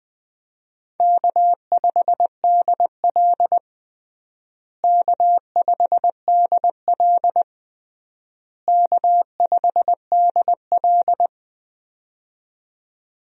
Morse code, heard as a keyed tone, about 20 wpm, 700 Hz. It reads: K5DL K5DL K5DL
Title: USA Call Signs - 4 Characters - Repeated 3x - 20wpm